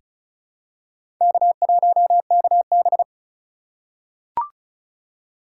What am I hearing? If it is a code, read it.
Morse code, heard as a keyed tone, about 33 wpm, 700 Hz. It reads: K1KB
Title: USA Call Signs - 4 Characters - 4x Character Spacing - 35wpm